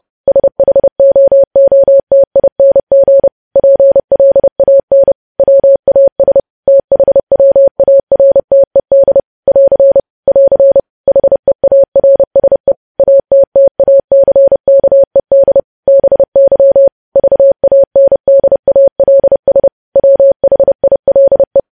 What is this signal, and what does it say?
Morse code, heard as a keyed tone, about 30 wpm, 550 Hz. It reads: SHOOTING PLAN WAS THWARTED <AR> <AR> HEARSE ATTACKED BY VANDALS WHILE
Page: news headlines in morse code at 30 wpm